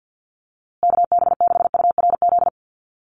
Morse code, heard as a keyed tone, about 50 wpm, 700 Hz. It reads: KB6URZ
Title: USA Call Signs Repeated 3x 50wpm